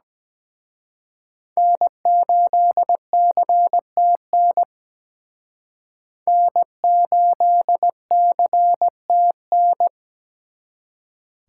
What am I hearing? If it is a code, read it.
Morse code, heard as a keyed tone, about 20 wpm, 700 Hz. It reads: N8CTN N8CTN
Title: Speed Racing - USA 1x3 Call Signs 17wpm